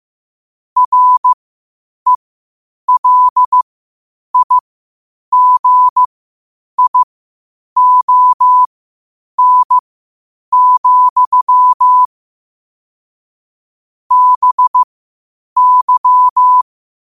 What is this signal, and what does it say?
Morse code, heard as a keyed tone, about 15 wpm, spaced as for 8 wpm, 1000 Hz. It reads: RELIGION, BY
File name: qotdfiles / 8wpm